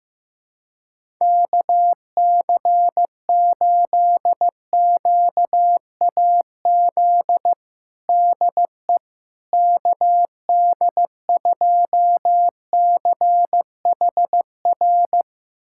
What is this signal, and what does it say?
Morse code, heard as a keyed tone, about 15 wpm, 700 Hz. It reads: KC8QAZ DE KD2CHR
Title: USA Callsign DE Callsign - 6 Characters - 15wpm